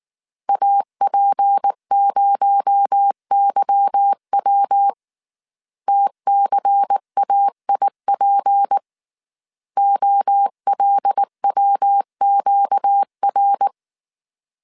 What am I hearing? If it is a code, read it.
Morse code, heard as a keyed tone, about 19 wpm, 800 Hz. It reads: AP0YW TCAIP OLWQR
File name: CW / DET / 1 9 1500C S A 18wpm 2459